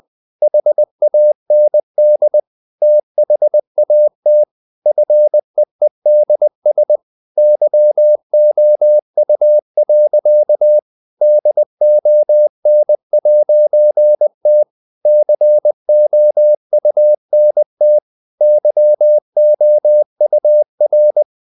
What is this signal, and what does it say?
Morse code, heard as a keyed tone, about 20 wpm, 600 Hz. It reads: HAND THAT FEEDS YOU. DON'T COUNT YOUR
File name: CW / Practice / EnglishSayings20